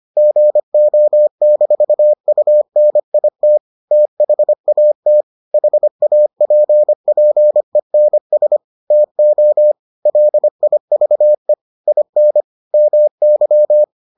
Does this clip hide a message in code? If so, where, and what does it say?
Yes: Morse code from 0.2 to 13.8 s, heard as a keyed tone, about 25 wpm, 600 Hz. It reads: GO-UNIT THAT HAPPENS TO LIVE IN MY